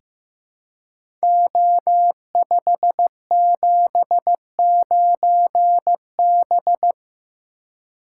Morse code, heard as a keyed tone, about 15 wpm, 700 Hz. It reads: O579B
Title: Code Groups - 5 Letter:Numbers - 15wpm